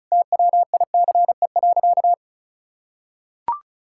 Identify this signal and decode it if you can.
Morse code, heard as a keyed tone, about 36 wpm, 700 Hz. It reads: TWICE.
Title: Sentences from Top 1600 Words 35wpm